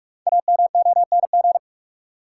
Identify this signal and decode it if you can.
Morse code, heard as a keyed tone, about 43 wpm, 700 Hz. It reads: AMONG
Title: Sets of 2 Words - New Word Drill - 45wpm